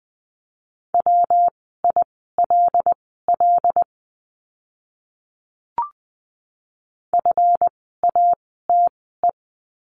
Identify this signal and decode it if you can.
Morse code, heard as a keyed tone, about 20 wpm, spaced as for 14 wpm, 700 Hz. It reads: WILL FATE
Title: 139 Lesson 23f - Letter f - Words 20wpm